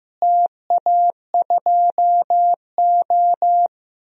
Morse code, heard as a keyed tone, about 15 wpm, 700 Hz. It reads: TA2O
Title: DXCC-100 Call Signs - 4 Characters - Repeated 3x - 15wpm